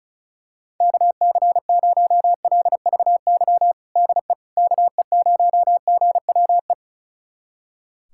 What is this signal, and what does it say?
Morse code, heard as a keyed tone, about 34 wpm, 700 Hz. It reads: KC0LVY DE KE0GWE
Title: USA Callsign DE Callsign 35wpm